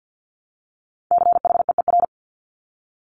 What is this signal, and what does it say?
Morse code, heard as a keyed tone, about 50 wpm, 700 Hz. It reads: CHEER
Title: Multi-letter Phonemes Repeated 3x 50wpm